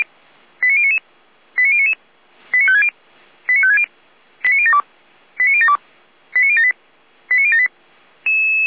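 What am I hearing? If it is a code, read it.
Five-tone selective call (ZVEI), heard as quick runs of five tones, tones of about 70 ms. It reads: 89099 89099 89560 89560 89072 89072 89078 89078
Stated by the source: ZVEI five-tone call received by SDR.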